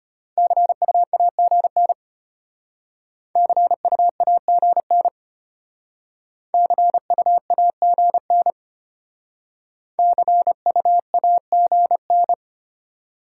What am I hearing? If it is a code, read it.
Morse code, heard as a keyed tone, about 32 wpm, 690 Hz. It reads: CUAGN CUAGN CUAGN CUAGN